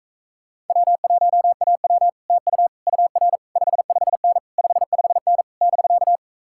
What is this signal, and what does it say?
Morse code, heard as a keyed tone, about 44 wpm, 700 Hz. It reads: W1AW TU UR 55N 55N <BK>